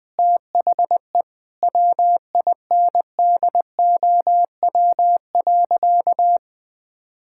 Morse code, heard as a keyed tone, about 20 wpm, 700 Hz. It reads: THE WINDOW.
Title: Sentences from Top 1600 Words 20wpm